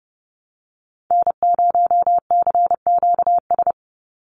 Morse code, heard as a keyed tone, about 30 wpm, 700 Hz. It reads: N0CQS
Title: USA Call Signs 30wpm